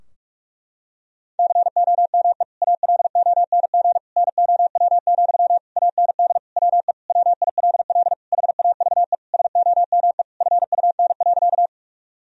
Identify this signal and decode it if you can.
Morse code, heard as a keyed tone, about 46 wpm, 700 Hz. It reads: COME ALONG NOW, AND WE WILL HAVE SOME FUN.
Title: Sentences from Top 1000 Words - Review Part 4 - 45wpm